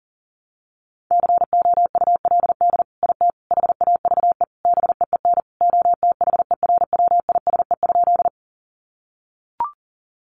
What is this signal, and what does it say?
Morse code, heard as a keyed tone, about 40 wpm, 700 Hz. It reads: COULD IT HAVE BEEN OTHERWISE?
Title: Sentences from Top 1200 Words 40wpm